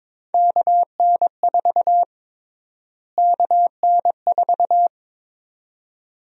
Morse code, heard as a keyed tone, about 22 wpm, 700 Hz. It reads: KN4 KN4
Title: USA Call Sign Prefixes - Repeated 2x - 22wpm